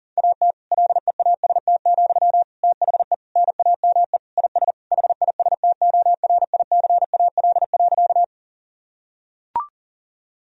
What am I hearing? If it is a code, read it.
Morse code, heard as a keyed tone, about 37 wpm, 700 Hz. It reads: AT LEAST, THE NAME IS HISTORICAL.
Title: Sentences from Top 1000 Words - Review Part 4 - 40wpm